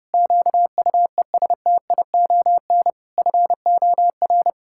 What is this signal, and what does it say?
Morse code, heard as a keyed tone, about 30 wpm, 700 Hz. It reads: QUESTION FOR
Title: Pangram Challenge v2 - 30wpm